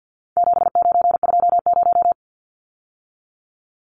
Morse code, heard as a keyed tone, about 50 wpm, 700 Hz. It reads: 7920
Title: Numbers - 4-Digit 50wpm